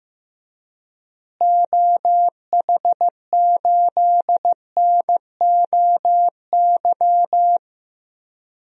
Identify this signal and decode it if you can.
Morse code, heard as a keyed tone, about 15 wpm, 700 Hz. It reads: OH8NOY
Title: DXCC-100 Call Signs - 6 Characters - Repeated 3x - 15wpm